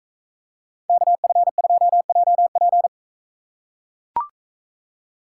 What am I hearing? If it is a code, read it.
Morse code, heard as a keyed tone, about 39 wpm, 700 Hz. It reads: KF2JP